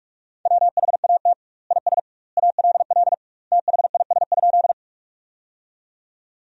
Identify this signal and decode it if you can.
Morse code, heard as a keyed tone, about 46 wpm, 700 Hz. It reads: WHAT IS ALL THIS?
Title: 2x Word-Spacing - Sentences from Top 100 Words - 45wpm